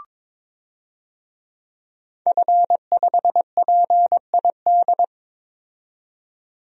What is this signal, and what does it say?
Morse code, heard as a keyed tone, about 21 wpm, 700 Hz. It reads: F5PID
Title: DXCC-100 Call Signs - 5 Characters - 22wpm